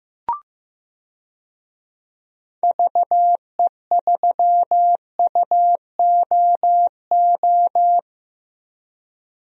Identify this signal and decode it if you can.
Morse code, heard as a keyed tone, about 16 wpm, 700 Hz. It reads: VE3UOO
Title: DXCC-100 Call Signs - 6 Characters - Repeated 3x - 15wpm